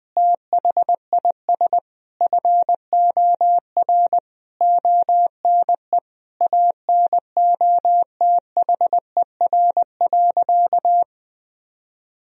Morse code, heard as a keyed tone, about 19 wpm, 700 Hz. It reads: THIS FOR ONE ANOTHER.
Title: Sentences from Top 1000 Words - Review Part 1 - 20wpm